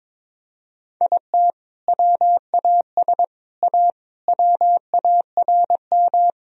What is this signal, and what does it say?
Morse code, heard as a keyed tone, about 22 wpm, 700 Hz. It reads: IT WAS A WARM
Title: Sentences from Top 1000 Words - Review Part 2 - 22wpm